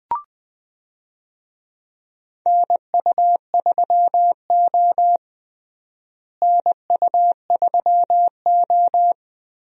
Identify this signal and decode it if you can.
Morse code, heard as a keyed tone, about 21 wpm, 700 Hz. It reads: NU3O NU3O
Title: USA Call Signs - 4 Characters - Repeated 3x - 20wpm